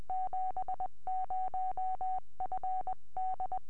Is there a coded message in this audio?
Yes: Morse code from 0.1 to 3.6 s, heard as a keyed tone, about 20 wpm, 750 Hz. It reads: 70FD